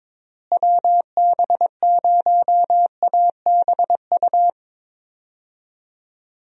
Morse code, heard as a keyed tone, about 22 wpm, 700 Hz. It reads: WB0ABU
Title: USA Call Signs - 6 Characters - Repeated 3x - 22wpm